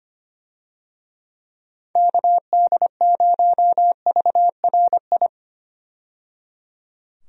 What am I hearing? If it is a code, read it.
Morse code, heard as a keyed tone, about 25 wpm, 700 Hz. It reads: KD0VRI
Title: USA Call Signs - 6 Characters - 25wpm